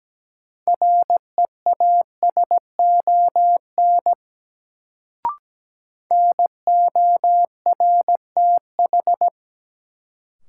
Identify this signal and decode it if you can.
Morse code, heard as a keyed tone, about 17 wpm, 700 Hz. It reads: REASON NORTH
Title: Sets of 5 Words Encore - New Word Drill - 17wpm